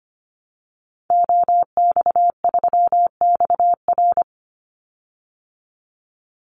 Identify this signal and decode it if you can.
Morse code, heard as a keyed tone, about 25 wpm, 700 Hz. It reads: OX3XR
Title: Call Signs Repeated 3x - 25wpm